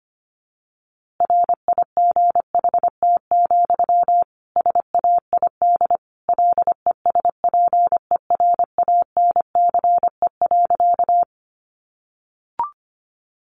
Morse code, heard as a keyed tone, about 25 wpm, 700 Hz. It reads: RIGHT, SAID LESPERANCE.
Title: A Sound Of Thunder 25wpm